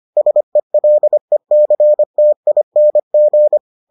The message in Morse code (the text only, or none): SELECTING